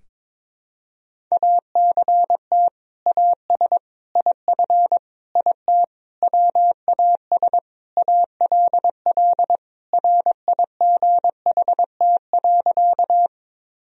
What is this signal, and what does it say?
Morse code, heard as a keyed tone, about 22 wpm, 700 Hz. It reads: ACT AS IF IT WAS ALL RIGHT.
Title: Sentences from Top 1000 Words - Review Part 1 - 22wpm